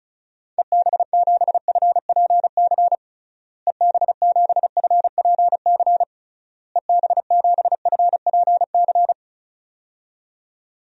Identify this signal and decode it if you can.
Morse code, heard as a keyed tone, about 34 wpm, 700 Hz. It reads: EB7FPC EB7FPC EB7FPC
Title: DXCC-100 Call Signs - 6 Characters - Repeated 3x - 35wpm